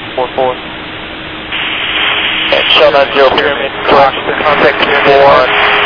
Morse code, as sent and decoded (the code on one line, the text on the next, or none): none